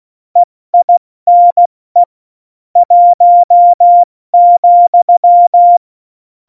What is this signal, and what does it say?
Morse code, heard as a keyed tone, about 16 wpm, 700 Hz. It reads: EINE 1,
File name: Morsekurs / Witze / Witz 22. Dezember 2020, 70 F80